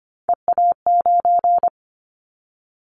Morse code, heard as a keyed tone, about 25 wpm, 700 Hz. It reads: EA9
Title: Callsign Prefixes for Top 100 DXCC - 25wpm